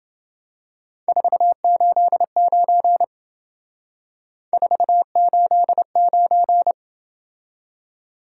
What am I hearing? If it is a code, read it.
Morse code, heard as a keyed tone, about 28 wpm, 700 Hz. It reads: 489 489